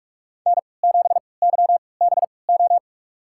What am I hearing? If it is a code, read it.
Morse code, heard as a keyed tone, about 46 wpm, spaced as for 25 wpm, 700 Hz. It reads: N7YBO
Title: USA Call Signs - 5 Characters - 3x Character Spacing - 45wpm